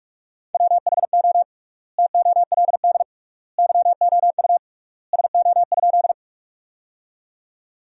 Morse code, heard as a keyed tone, about 46 wpm, 700 Hz. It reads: WHO TOLD YOU SO?
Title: 3x Word-Spacing - Sentences from Top 400 Words - 45wpm